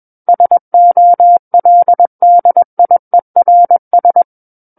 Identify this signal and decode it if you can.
Morse code, heard as a keyed tone, about 21 wpm, 700 Hz. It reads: SOLDIERS
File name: Book One / 12-Chapter 12 - 21 WPM